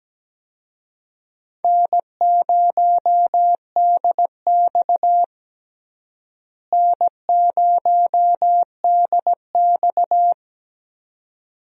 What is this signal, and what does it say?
Morse code, heard as a keyed tone, about 17 wpm, 700 Hz. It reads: N0DX N0DX
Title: USA Call Signs - 4 Characters - Repeated 2x - 17wpm